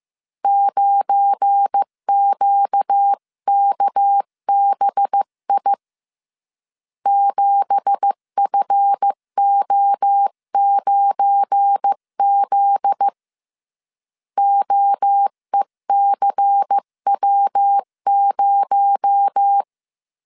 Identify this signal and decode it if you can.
Morse code, heard as a keyed tone, about 15 wpm, 800 Hz. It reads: 9QKBI 7FO9Z OECW0
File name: CW / DET / 1 9 1500C S B 14wpm 3302